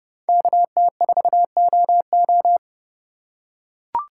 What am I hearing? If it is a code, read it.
Morse code, heard as a keyed tone, about 30 wpm, 700 Hz. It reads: KT4OO
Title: USA Call Signs - 5 Characters - 3x Character Spacing - 30wpm